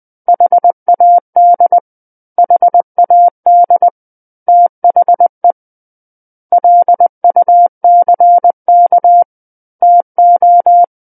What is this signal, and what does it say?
Morse code, heard as a keyed tone, about 20 wpm, 700 Hz. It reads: HAD HAD THE LUCK TO
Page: SKCC Learning Center - Morse Code Practice Files